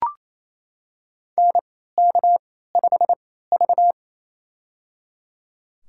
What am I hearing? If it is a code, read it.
Morse code, heard as a keyed tone, about 28 wpm, spaced as for 16 wpm, 700 Hz. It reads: NK5V
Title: USA Call Signs - 3x Character Spacing - 28wpm